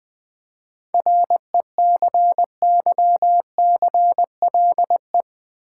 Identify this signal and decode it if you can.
Morse code, heard as a keyed tone, about 20 wpm, 700 Hz. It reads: RECYCLE
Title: 7 Letter Words - First 3 Letters Given - 20wpm